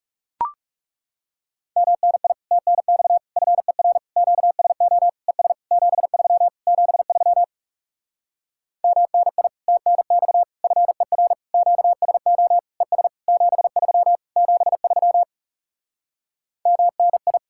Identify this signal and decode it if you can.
Morse code, heard as a keyed tone, about 40 wpm, 700 Hz. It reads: MNI TNX FER QSO ES 73 73 MNI TNX FER QSO ES 73 73 MNI